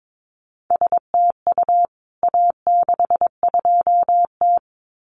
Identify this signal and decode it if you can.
Morse code, heard as a keyed tone, about 22 wpm, 700 Hz. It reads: STU A62T
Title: CWT Exchange 22wpm